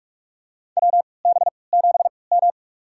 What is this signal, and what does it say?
Morse code, heard as a keyed tone, about 46 wpm, spaced as for 25 wpm, 700 Hz. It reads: WB7M
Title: USA Call Signs - 3x Character Spacing - 45wpm